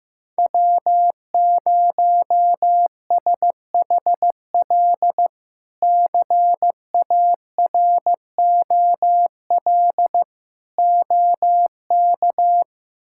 Morse code, heard as a keyed tone, about 15 wpm, 700 Hz. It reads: W0SHL CAROL OK
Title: K1USN SST QSO Practice - 15wpm